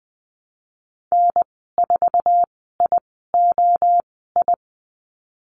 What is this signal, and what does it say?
Morse code, heard as a keyed tone, about 20 wpm, spaced as for 15 wpm, 700 Hz. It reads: N4IOI
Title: 076 Lesson 13g - Letter L - Call Signs 20wpm